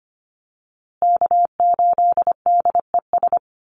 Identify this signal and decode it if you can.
Morse code, heard as a keyed tone, about 25 wpm, 700 Hz. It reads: K8DES
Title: USA Call Signs Repeated 3x 25wpm